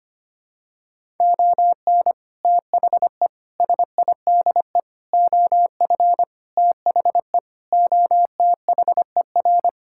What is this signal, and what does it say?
Morse code, heard as a keyed tone, about 25 wpm, 700 Hz. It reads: ON THE SIDE OF THE OTHER
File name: practice / 25wpm / Sentences with 3-Word Phrases - 25wpm